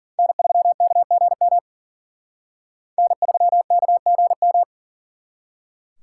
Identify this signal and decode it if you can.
Morse code, heard as a keyed tone, about 49 wpm, 700 Hz. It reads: N3KGM N3KGM
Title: Speed Racing - USA 1x3 Call Signs 40wpm